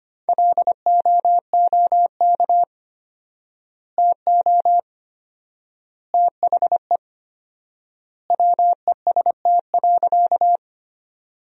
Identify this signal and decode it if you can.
Morse code, heard as a keyed tone, about 25 wpm, 700 Hz. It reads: LOOK TO THE WEST.
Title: 4x Word-Spacing - Sentences from Top 400 Words - 25wpm